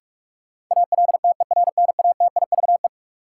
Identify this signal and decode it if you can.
Morse code, heard as a keyed tone, about 46 wpm, 700 Hz. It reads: ALTERNATIVE
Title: Sentences from Top 1000 Words - Review Part 4 - New Word Drill - 45wpm